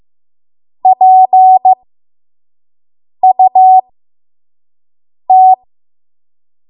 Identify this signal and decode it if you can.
Morse code, heard as a keyed tone, about 15 wpm, spaced as for 5 wpm, 750 Hz. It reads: PUT
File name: Morse / 05WPM